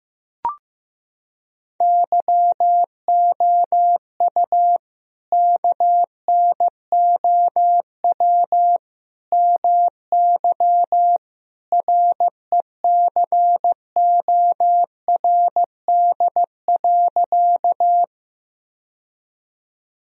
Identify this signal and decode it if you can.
Morse code, heard as a keyed tone, about 15 wpm, 700 Hz. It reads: YOU KNOW MY RECORD.
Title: Sentences from Top 1000 Words - Review Part 3 - 15wpm